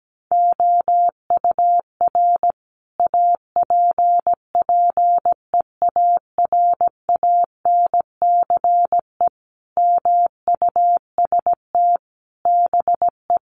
Morse code, heard as a keyed tone, about 17 wpm, 700 Hz. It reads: OUR APPEARANCE MUST BE
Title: Sentences from Top 1100 Words 17wpm